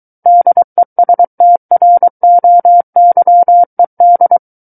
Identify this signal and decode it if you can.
Morse code, heard as a keyed tone, about 23 wpm, 700 Hz. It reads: DESTROYED
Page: SKCC Learning Center - Morse Code Practice Files